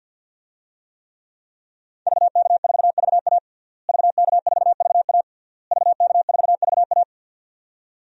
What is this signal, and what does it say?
Morse code, heard as a keyed tone, about 47 wpm, 700 Hz. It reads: VK4VA VK4VA VK4VA
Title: DXCC-100 Call Signs - 5 Characters - Repeated 3x - 50wpm